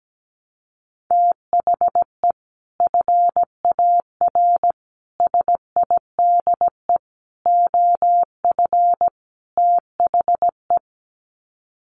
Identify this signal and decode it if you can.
Morse code, heard as a keyed tone, about 17 wpm, 700 Hz. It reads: THE FAR SIDE OF THE
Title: Sets of 5 Words 17wpm